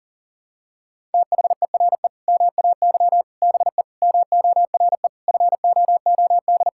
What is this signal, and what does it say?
Morse code, heard as a keyed tone, about 40 wpm, 700 Hz. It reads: THERE MAY BE MORE FOOD
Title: Sentences from Top 1000 Words - Review Part 6 - 40wpm